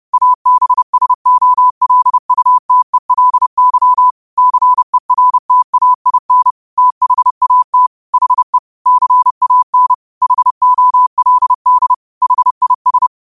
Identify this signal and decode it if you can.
Morse code, heard as a keyed tone, about 30 wpm, 1000 Hz. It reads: ABSOLUTELY CERTAIN THAT HE CAN HOLD HIS